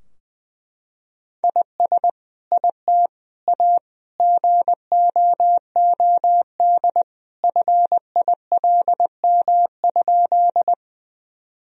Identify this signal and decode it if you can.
Morse code, heard as a keyed tone, about 20 wpm, 700 Hz. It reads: IS IT A GOOD FILM?
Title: Sentences from Top 1000 Words - Review Part 3 - 20wpm